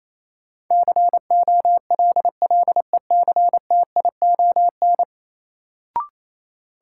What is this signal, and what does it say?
Morse code, heard as a keyed tone, about 28 wpm, 700 Hz. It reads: COLLECTION
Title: Sentences from Top 1200 Words - New Word Drill - 28wpm